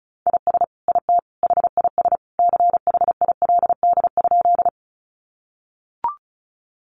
Morse code, heard as a keyed tone, about 34 wpm, 700 Hz. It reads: IS IT HIS CHILD?
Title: Sentences from Top 800 Words 35wpm